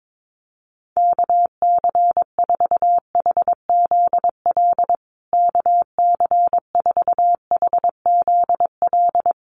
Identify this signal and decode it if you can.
Morse code, heard as a keyed tone, about 22 wpm, 700 Hz. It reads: KC4HZL KC4HZL
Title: USA Call Signs Repeated 3x 22wpm